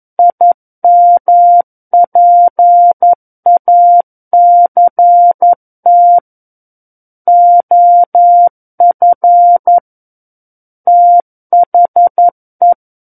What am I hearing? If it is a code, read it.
Morse code, heard as a keyed tone, about 11 wpm, 700 Hz. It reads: IMPACT OF THE
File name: Book One / 02-Chapter 2 - 11 WPM